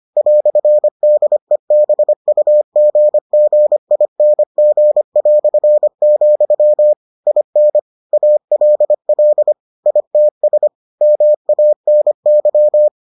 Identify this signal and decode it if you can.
Morse code, heard as a keyed tone, about 25 wpm, 600 Hz. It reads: "DEBUGGING", IN ALL ITS MANY